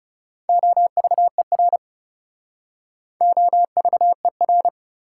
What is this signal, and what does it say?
Morse code, heard as a keyed tone, about 40 wpm, 690 Hz. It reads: OVER OVER